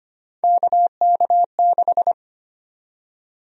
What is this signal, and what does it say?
Morse code, heard as a keyed tone, about 25 wpm, 700 Hz. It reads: KK6
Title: USA Call Sign Prefixes - 25wpm